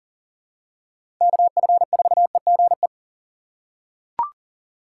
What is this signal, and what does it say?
Morse code, heard as a keyed tone, about 40 wpm, 700 Hz. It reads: KF4EGE
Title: USA Call Signs - 6 Characters - 4x Character Spacing - 40wpm